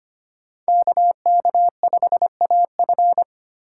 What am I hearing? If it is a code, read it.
Morse code, heard as a keyed tone, about 32 wpm, 700 Hz. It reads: KK5AF